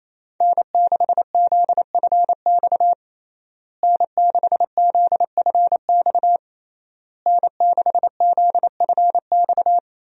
Morse code, heard as a keyed tone, about 28 wpm, 700 Hz. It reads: N6ZFX N6ZFX N6ZFX
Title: USA Call Signs - 5 Characters - Repeated 3x - 28wpm